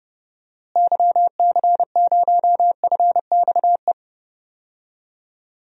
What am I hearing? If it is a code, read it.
Morse code, heard as a keyed tone, about 30 wpm, 700 Hz. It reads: YC0FXE
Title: DXCC-100 Call Signs - 6 Characters - Repeated 2x - 30wpm